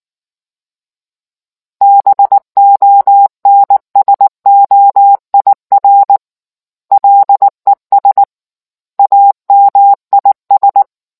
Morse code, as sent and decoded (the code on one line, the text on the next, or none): -... --- -. ... --- .. .-. / .-.. ... / .- -- .. ...
BONSOIR LES AMIS